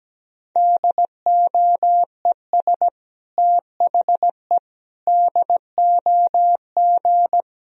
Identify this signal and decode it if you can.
Morse code, heard as a keyed tone, about 17 wpm, 700 Hz. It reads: DOES THE DOG
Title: Sentences from Top 1000 Words - Review Part 3 - 17wpm